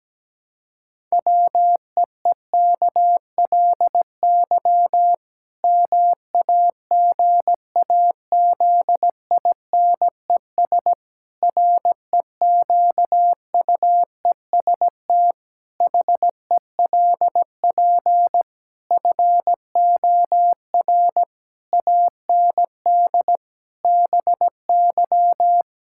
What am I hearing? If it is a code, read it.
Morse code, heard as a keyed tone, about 17 wpm, 700 Hz. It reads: WEEKLY MAGAZINES REQUEST HELP FOR AND BY
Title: Pangram Challenge v2 - 17wpm